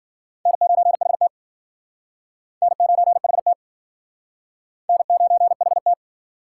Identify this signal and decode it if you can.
Morse code, heard as a keyed tone, about 55 wpm, 700 Hz. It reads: N9HT N9HT N9HT